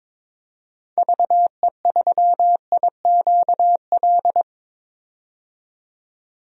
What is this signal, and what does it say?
Morse code, heard as a keyed tone, about 22 wpm, 700 Hz. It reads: VE3IQL
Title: DXCC-100 Call Signs - 6 Characters - 22wpm